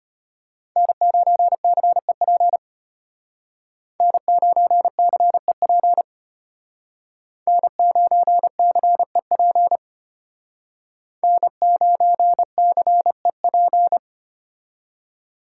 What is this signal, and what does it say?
Morse code, heard as a keyed tone, about 32 wpm, 700 Hz. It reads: N9CEP N9CEP N9CEP N9CEP